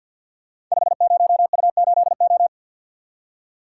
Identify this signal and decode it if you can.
Morse code, heard as a keyed tone, about 46 wpm, 690 Hz. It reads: 50U8O
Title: Code Groups - 5 Letter:Numbers - 50wpm